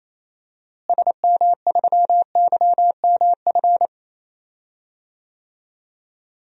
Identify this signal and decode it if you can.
Morse code, heard as a keyed tone, about 28 wpm, 700 Hz. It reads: SM3YMF
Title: DXCC-100 Call Signs - 6 Characters - Repeated 2x - 28wpm